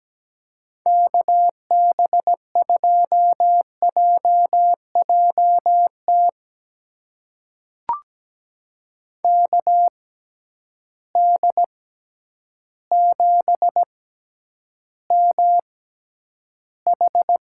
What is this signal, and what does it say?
Morse code, heard as a keyed tone, about 17 wpm, spaced as for 9 wpm, 700 Hz. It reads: KB2JJT KD7MH
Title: USA Call Signs - 6 Characters - 6x Character Spacing - 17wpm